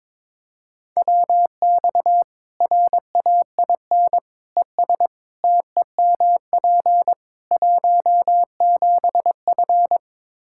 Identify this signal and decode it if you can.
Morse code, heard as a keyed tone, about 22 wpm, 700 Hz. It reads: WX RAIN ES TEMP 17F
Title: Practice QSO - Exchange 2 - Info - 22wpm